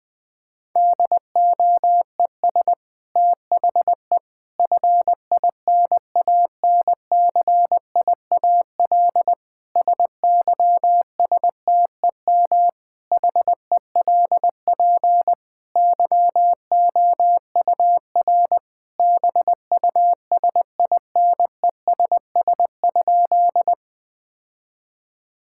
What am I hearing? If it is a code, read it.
Morse code, heard as a keyed tone, about 20 wpm, 700 Hz. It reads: DOES THE FINANCIAL SYSTEM HELP YOUR BUSINESS?
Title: Sentences from Top 1000 Words - Review Part 5 - 20wpm